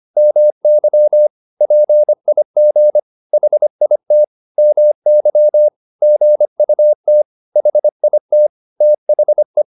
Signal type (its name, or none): Morse code